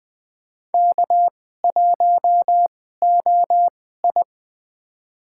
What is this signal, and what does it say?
Morse code, heard as a keyed tone, about 20 wpm, spaced as for 14 wpm, 700 Hz. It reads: K1OI
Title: USA Call Signs - 4 Characters - 2x Character Spacing - 20wpm